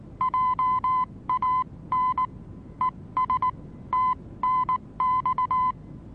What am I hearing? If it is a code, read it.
Morse code, heard as a keyed tone, about 19 wpm, spaced as for 16 wpm, 1020 Hz. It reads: JAN ES TNX
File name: repos / Morsekurs / DJ4UF Morsen26